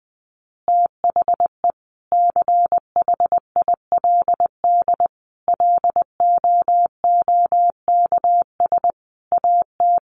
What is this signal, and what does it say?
Morse code, heard as a keyed tone, about 19 wpm, 700 Hz. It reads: THE CHILD LOOKS AT
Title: Additional Easy Phrases 20wpm